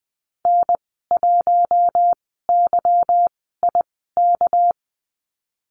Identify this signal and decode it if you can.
Morse code, heard as a keyed tone, about 19 wpm, 700 Hz. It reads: N 1 Y I K
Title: 238 Lesson 38g - Letter q - Call Signs 20wpm